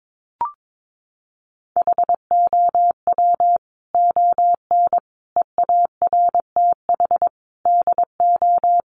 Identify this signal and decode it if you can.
Morse code, heard as a keyed tone, about 22 wpm, 700 Hz. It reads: HOW ON EARTH DO
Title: Sentences from Top 200 Words 22wpm